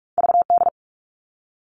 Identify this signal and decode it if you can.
Morse code, heard as a keyed tone, about 46 wpm, 700 Hz. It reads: VD
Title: All Cut Numbers - 2-Digit 45wpm